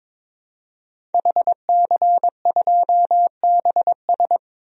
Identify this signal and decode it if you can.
Morse code, heard as a keyed tone, about 22 wpm, 700 Hz. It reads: HC2BS
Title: DXCC-100 Call Signs - 5 Characters - 22wpm